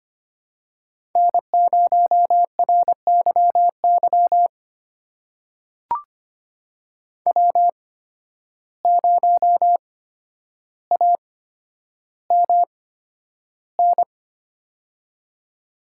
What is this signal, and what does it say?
Morse code, heard as a keyed tone, about 25 wpm, spaced as for 11 wpm, 700 Hz. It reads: N0RYY W0AMN
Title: USA Call Signs - 5 Characters - 8x Character Spacing - 25wpm